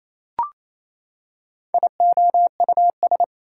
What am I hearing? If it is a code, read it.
Morse code, heard as a keyed tone, about 28 wpm, 700 Hz. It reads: IOUS
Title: Prefixes and Suffixes 28wpm